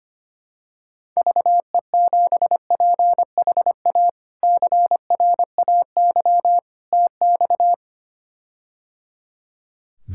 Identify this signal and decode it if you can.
Morse code, heard as a keyed tone, about 25 wpm, 700 Hz. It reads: VE7PHA CRAY TX
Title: K1USN SST QSO Practice - 25wpm